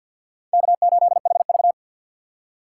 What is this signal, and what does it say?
Morse code, heard as a keyed tone, about 49 wpm, 700 Hz. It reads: K8HV
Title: USA Call Signs - 4 Characters - 6x Character Spacing - 50wpm